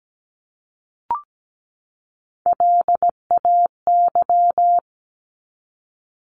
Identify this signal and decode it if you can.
Morse code, heard as a keyed tone, about 17 wpm, 700 Hz. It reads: LAY
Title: Top 500 Words 17wpm